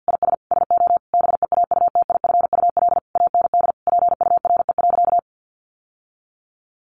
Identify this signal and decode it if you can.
Morse code, heard as a keyed tone, about 50 wpm, 700 Hz. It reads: IS SO BEAUTIFUL AND PURE.
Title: Sentences from Top 1500 Words 50wpm